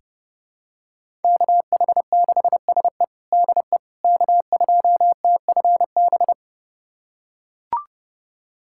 Morse code, heard as a keyed tone, about 32 wpm, 700 Hz. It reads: KH6SE DE K2TFB
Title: USA Callsign DE Callsign - 5 Characters - 30wpm